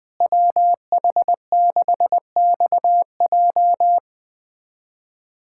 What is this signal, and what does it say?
Morse code, heard as a keyed tone, about 21 wpm, 700 Hz. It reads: WH6XJ
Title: USA Callsign DE Callsign - 5 Characters - 20wpm